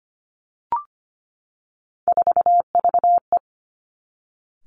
Morse code, heard as a keyed tone, about 25 wpm, 700 Hz. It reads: 4VE
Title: Three Challenging Letter-Number Combinations 25wpm